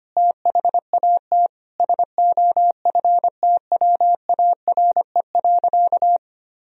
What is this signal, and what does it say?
Morse code, heard as a keyed tone, about 25 wpm, 700 Hz. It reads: THAT SOFTWARE.
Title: Sentences from Top 1000 Words - Review Part 3 - 25wpm